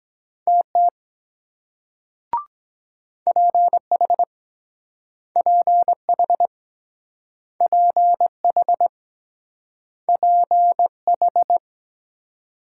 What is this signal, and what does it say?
Morse code, heard as a keyed tone, about 23 wpm, 700 Hz. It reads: TT PH PH PH PH